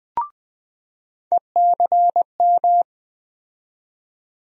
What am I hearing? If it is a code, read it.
Morse code, heard as a keyed tone, about 20 wpm, 700 Hz. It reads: ECM
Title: USA Call Sign Postfixes - 3 Letters 20wpm